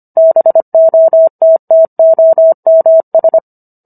Morse code, heard as a keyed tone, about 25 wpm, 650 Hz. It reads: BOTTOMS